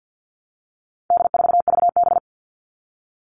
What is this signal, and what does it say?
Morse code, heard as a keyed tone, about 50 wpm, 700 Hz. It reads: D4VB
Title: All Cut Numbers - 4-Digit 50wpm